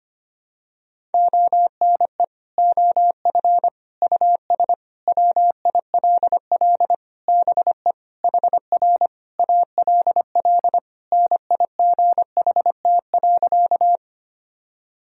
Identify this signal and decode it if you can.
Morse code, heard as a keyed tone, about 25 wpm, 700 Hz. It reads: ONE OF US WILL BE HR ALL NIGHT.
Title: Sentences from Top 300 Words and Abbreviations 25wpm